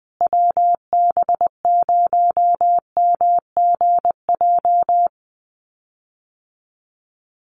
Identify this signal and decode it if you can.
Morse code, heard as a keyed tone, about 20 wpm, 700 Hz. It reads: WB0MGJ
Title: USA Call Signs Repeated 2x 20wpm